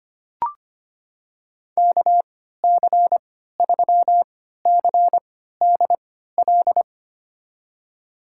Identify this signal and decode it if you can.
Morse code, heard as a keyed tone, about 25 wpm, spaced as for 14 wpm, 700 Hz. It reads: KC3CDL
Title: USA Call Signs - 3x Character Spacing - 25wpm